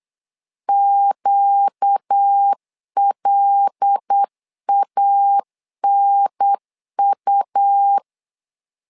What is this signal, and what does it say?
Morse code, heard as a keyed tone, about 8 wpm, 800 Hz. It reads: QLANU
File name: LEZIONI / 5 LBVC / 5 L 8wpm 2831